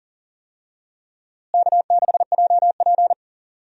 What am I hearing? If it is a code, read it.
Morse code, heard as a keyed tone, about 37 wpm, 700 Hz. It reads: K6JP
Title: USA Call Signs - 4 Characters - Repeated 3x - 40wpm